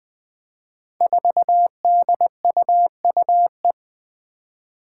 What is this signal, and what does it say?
Morse code, heard as a keyed tone, about 20 wpm, 700 Hz. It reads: 4DUUE
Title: Five Challenging Letter-Number Combinations 20wpm